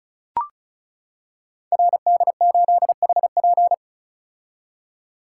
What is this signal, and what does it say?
Morse code, heard as a keyed tone, about 34 wpm, 700 Hz. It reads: RD8HP
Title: Code Groups - 5 Letter:Numbers - 35wpm